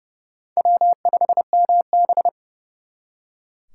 Morse code, heard as a keyed tone, about 39 wpm, 700 Hz. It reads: W5MB